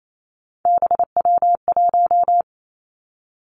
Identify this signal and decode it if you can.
Morse code, heard as a keyed tone, about 28 wpm, 700 Hz. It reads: BW1
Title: Three Challenging Letter-Number Combinations 28wpm